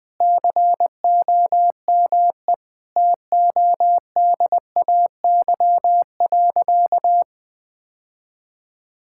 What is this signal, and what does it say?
Morse code, heard as a keyed tone, about 21 wpm, 700 Hz. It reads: COME TODAY.
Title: Sentences with 3-Word Phrases - 20wpm